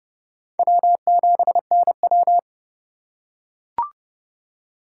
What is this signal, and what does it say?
Morse code, heard as a keyed tone, about 27 wpm, 700 Hz. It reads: W7NW